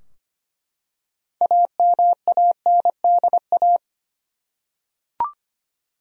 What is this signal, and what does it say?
Morse code, heard as a keyed tone, about 25 wpm, 700 Hz. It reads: AMANDA
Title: USA Common Names - 25wpm